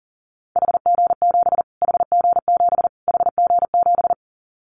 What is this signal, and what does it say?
Morse code, heard as a keyed tone, about 40 wpm, 700 Hz. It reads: HG7 HG7 HG7
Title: Callsign Prefixes for Top 25 DXCC - Repeated 3x - 40wpm